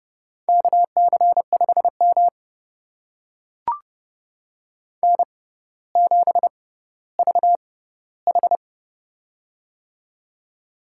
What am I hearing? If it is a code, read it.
Morse code, heard as a keyed tone, about 30 wpm, spaced as for 15 wpm, 700 Hz. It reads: KC5M N7VH
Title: USA Call Signs - 4 Characters - 6x Character Spacing - 30wpm